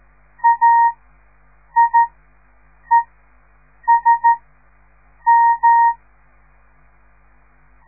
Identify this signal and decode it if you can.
Morse code, heard as a keyed tone, about 13 wpm, spaced as for 7 wpm, 930 Hz. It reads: AIESM
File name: radiomorningstar / morse / les 6